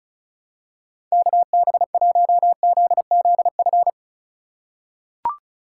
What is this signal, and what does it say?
Morse code, heard as a keyed tone, about 36 wpm, 700 Hz. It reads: KB1ZZF
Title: USA Call Signs - 2x Character Spacing - 35wpm